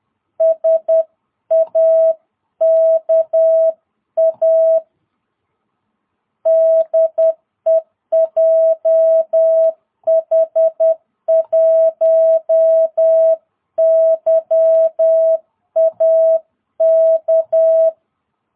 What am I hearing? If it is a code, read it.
Morse code, heard as a keyed tone, about 10 wpm, 660 Hz. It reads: SAKA DE JH1YAK
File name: jh8cbh / 2022cw